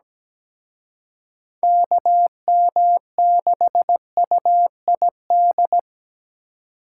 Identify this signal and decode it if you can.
Morse code, heard as a keyed tone, about 20 wpm, 700 Hz. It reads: KM6UID
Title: Speed Racing - USA 2x3 Call Signs 17wpm